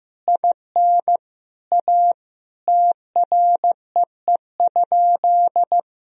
Morse code, heard as a keyed tone, about 15 wpm, 700 Hz. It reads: IN A TREE?
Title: Sentences from Top 1000 Words - Review Part 1 - 15wpm